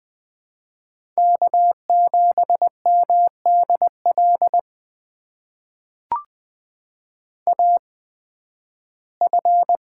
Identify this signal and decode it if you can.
Morse code, heard as a keyed tone, about 20 wpm, spaced as for 13 wpm, 700 Hz. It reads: K7MDL AF
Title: USA Call Signs - 5 Characters - 8x Character Spacing - 20wpm